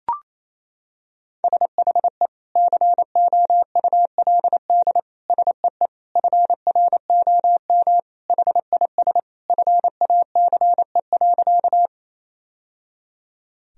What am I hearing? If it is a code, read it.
Morse code, heard as a keyed tone, about 27 wpm, 700 Hz. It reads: SHE COULD SEE FROM HIS FACE.
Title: Sentences from Top 1000 Words - Review Part 2 - 28wpm